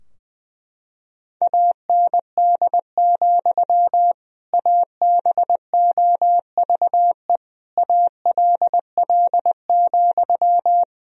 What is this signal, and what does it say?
Morse code, heard as a keyed tone, about 20 wpm, 700 Hz. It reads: AND, ABOVE ALL,
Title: Sentences from Top 1000 Words - Review Part 2 - 20wpm